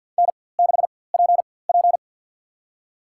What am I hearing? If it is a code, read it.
Morse code, heard as a keyed tone, about 47 wpm, spaced as for 23 wpm, 700 Hz. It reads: N6PP
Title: USA Call Signs - 4x Character Spacing - 50wpm